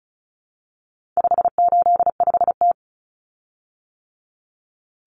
Morse code, heard as a keyed tone, about 34 wpm, 700 Hz. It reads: 585T
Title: Common Cut Numbers - 4-Digit 35wpm